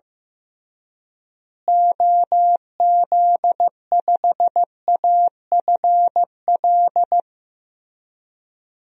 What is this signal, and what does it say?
Morse code, heard as a keyed tone, about 15 wpm, 700 Hz. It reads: OZ5AFL
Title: DXCC-100 Call Signs - 6 Characters - Repeated 3x - 15wpm